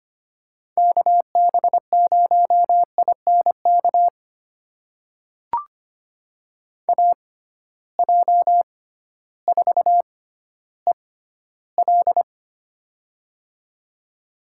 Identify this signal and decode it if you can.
Morse code, heard as a keyed tone, about 25 wpm, spaced as for 13 wpm, 700 Hz. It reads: KB0INK AJ4EL
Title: USA Call Signs - 6x Character Spacing - 25wpm